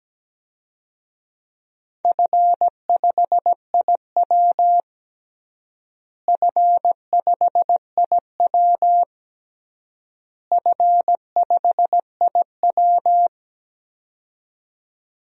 Morse code, heard as a keyed tone, about 17 wpm, 700 Hz. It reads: F5IW F5IW F5IW
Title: DXCC-100 Call Signs - 4 Characters - Repeated 3x - 17wpm